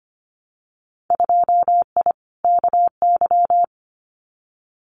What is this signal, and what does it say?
Morse code, heard as a keyed tone, about 25 wpm, 700 Hz. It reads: 2I KY
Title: Winter Field Day Exchanges 25wpm